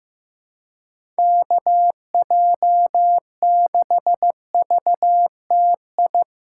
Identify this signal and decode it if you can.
Morse code, heard as a keyed tone, about 16 wpm, 700 Hz. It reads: KJ6VTI
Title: USA Callsign DE Callsign - 6 Characters - 15wpm